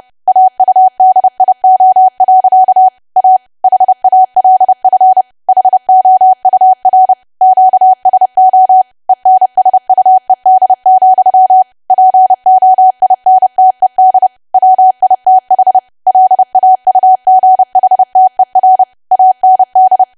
Morse code, heard as a keyed tone, about 30 wpm, 750 Hz. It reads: AUDIO. A HALF HOUR QSO ENSUED, POINTED WITH LAUGHTER AND
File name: cwa / W1AW-Level3 / 30WPM